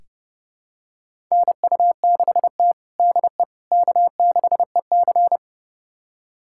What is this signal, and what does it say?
Morse code, heard as a keyed tone, about 32 wpm, 700 Hz. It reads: NU6T DE K6EC
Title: USA Callsign DE Callsign - 4 Characters - 30wpm